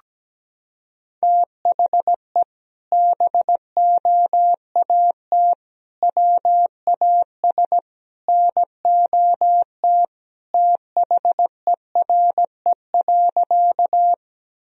Morse code, heard as a keyed tone, about 17 wpm, 700 Hz. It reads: THE BOAT WAS NOT THERE.
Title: Sentences from Top 1000 Words - Review Part 3 - 17wpm